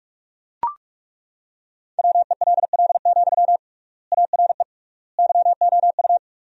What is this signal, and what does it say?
Morse code, heard as a keyed tone, about 41 wpm, 700 Hz. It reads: WELL, ARE YOU
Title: 3x Word-Spacing - Sentences from Top 400 Words - 45wpm